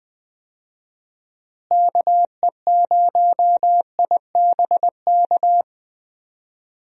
Morse code, heard as a keyed tone, about 20 wpm, 700 Hz. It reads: KE0IBK
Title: USA Call Signs - 6 Characters - Repeated 2x - 20wpm